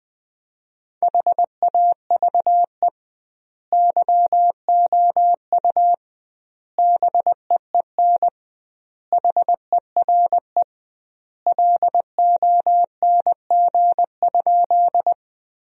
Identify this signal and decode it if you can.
Morse code, heard as a keyed tone, about 20 wpm, 700 Hz. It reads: HAVE YOU BEEN HERE LONG?
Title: 2x Word-Spacing - Sentences from Top 200 Words - 20wpm